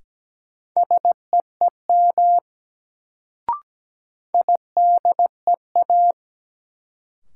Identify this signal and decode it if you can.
Morse code, heard as a keyed tone, about 17 wpm, 700 Hz. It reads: SEEM IDEA
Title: Sentences from Top 300 Words - New Word Drill - 17wpm